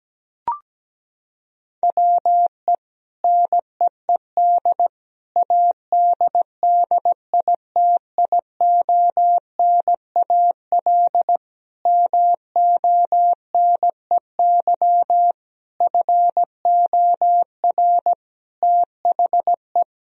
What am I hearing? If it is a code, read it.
Morse code, heard as a keyed tone, about 17 wpm, 700 Hz. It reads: WE NEED ADDITIONAL MONEY FOR THE
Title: Sentences from Top 1000 Words - Review Part 4 - 17wpm